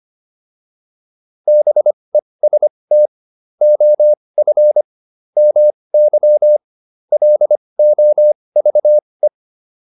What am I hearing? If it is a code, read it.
Morse code, heard as a keyed tone, about 25 wpm, spaced as for 20 wpm, 600 Hz. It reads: BEST OF MY LOVE
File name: PR302 20